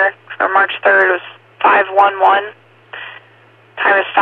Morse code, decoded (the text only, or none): none